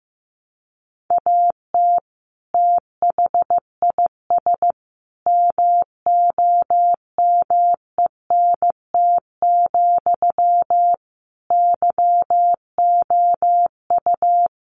Morse code, heard as a keyed tone, about 15 wpm, 700 Hz. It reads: AT THIS MOMENT, YOU
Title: Sentences from Top 1000 Words 15wpm